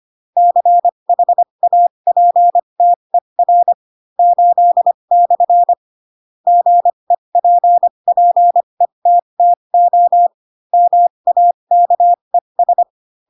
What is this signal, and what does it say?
Morse code, heard as a keyed tone, about 25 wpm, spaced as for 22 wpm, 700 Hz. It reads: CHAPTER 8/ GEPPETTO MAKES
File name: Pinocchio / Pinocchio ch8 18 25 ik1hge